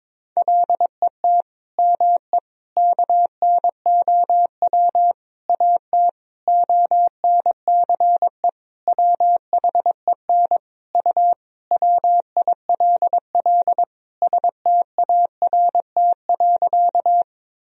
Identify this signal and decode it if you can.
Morse code, heard as a keyed tone, about 22 wpm, 700 Hz. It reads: LET ME KNOW AT ONCE WHEN U WILL START.
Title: Sentences from Top 300 Words and Abbreviations 22wpm